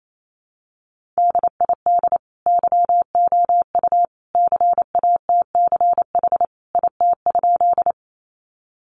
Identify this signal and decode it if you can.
Morse code, heard as a keyed tone, about 28 wpm, 700 Hz. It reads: DID YOU CATCH IT?
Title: Sentences from Top 1500 Words 28wpm